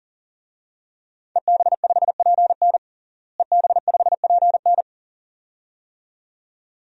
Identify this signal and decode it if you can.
Morse code, heard as a keyed tone, about 40 wpm, 700 Hz. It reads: EB5PN EB5PN
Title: DXCC-100 Call Signs - 5 Characters - Repeated 2x - 40wpm